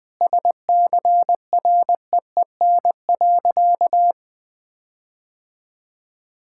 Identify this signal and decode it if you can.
Morse code, heard as a keyed tone, about 20 wpm, 700 Hz. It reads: SCREEN.
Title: Sentences with 6-Word Phrases - 20wpm